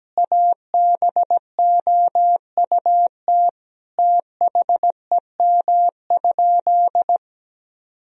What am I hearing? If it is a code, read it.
Morse code, heard as a keyed tone, about 17 wpm, 700 Hz. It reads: ABOUT THEM?
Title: Sentences from Top 1000 Words - Review Part 1 - 17wpm